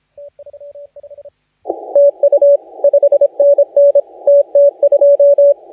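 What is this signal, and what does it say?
Morse code, heard as a keyed tone, about 24 wpm, 560 Hz, with radio noise. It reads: TU 5NN TT2
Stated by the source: Morse contest audio recorded by LA4C during the SAC CW 2018 contest.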